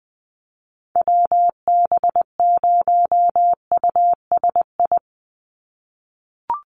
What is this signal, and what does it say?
Morse code, heard as a keyed tone, about 20 wpm, 700 Hz. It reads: WB0USI
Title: NA SOTA Activator Callsigns 20wpm